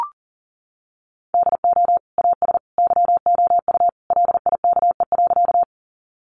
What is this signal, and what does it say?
Morse code, heard as a keyed tone, about 40 wpm, 700 Hz. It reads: DO AS YOU LIKE.
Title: Sentences from Top 100 Words 40wpm